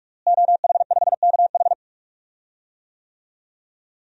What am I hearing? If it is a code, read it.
Morse code, heard as a keyed tone, about 46 wpm, 700 Hz. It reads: OH5KH
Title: DXCC-100 Call Signs - 5 Characters - Repeated 2x - 45wpm